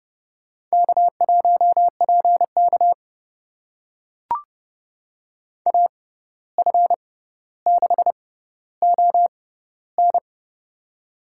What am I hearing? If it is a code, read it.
Morse code, heard as a keyed tone, about 32 wpm, spaced as for 15 wpm, 700 Hz. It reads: K1PK AF6ON
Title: USA Call Signs - 6x Character Spacing - 30wpm